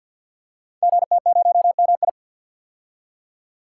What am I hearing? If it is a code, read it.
Morse code, heard as a keyed tone, about 50 wpm, 700 Hz. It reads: GT0MI
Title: Code Groups - 5 Letter:Numbers - 50wpm